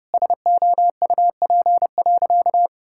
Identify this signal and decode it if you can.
Morse code, heard as a keyed tone, about 30 wpm, 700 Hz. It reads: SOUP.
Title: Sentences from Top 1000 Words - Review Part 6 - 30wpm